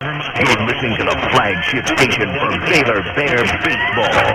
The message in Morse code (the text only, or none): none